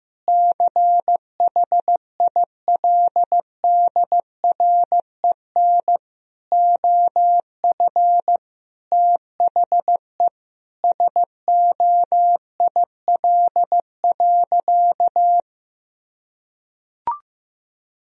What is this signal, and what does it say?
Morse code, heard as a keyed tone, about 15 wpm, 700 Hz. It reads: CHILDREN OF THE SOIL.
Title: Sentences from Top 1000 Words - Review Part 4 - 15wpm